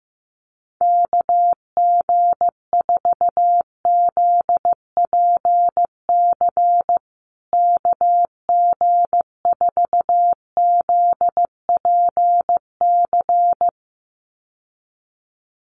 Morse code, heard as a keyed tone, about 15 wpm, 700 Hz. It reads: KG4ZPC KG4ZPC
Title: USA Call Signs Repeated 2x 15wpm